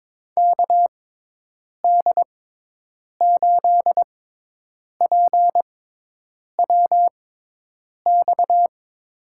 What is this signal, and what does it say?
Morse code, heard as a keyed tone, about 22 wpm, spaced as for 8 wpm, 700 Hz. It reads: KD8PWX
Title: USA Call Signs - 6 Characters - 6x Character Spacing - 22wpm